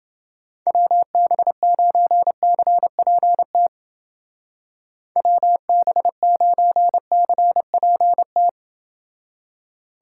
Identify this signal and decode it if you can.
Morse code, heard as a keyed tone, about 28 wpm, 700 Hz. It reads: WB9CPT WB9CPT